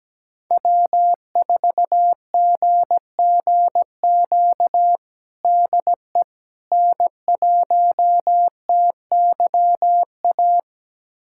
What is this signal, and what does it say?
Morse code, heard as a keyed tone, about 17 wpm, 700 Hz. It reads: W4GGQ DE N1TYA
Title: USA Callsign DE Callsign - 5 Characters - 17wpm